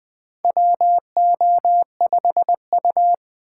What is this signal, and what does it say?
Morse code, heard as a keyed tone, about 20 wpm, 700 Hz. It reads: WO5U
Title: USA Call Signs - 4 Characters - 8x Character Spacing - 20wpm